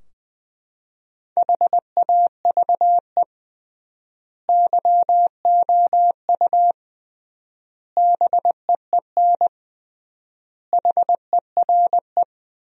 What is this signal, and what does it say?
Morse code, heard as a keyed tone, about 20 wpm, 700 Hz. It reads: HAVE YOU BEEN HERE
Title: 3x Word-Spacing - Sentences from Top 200 Words - 20wpm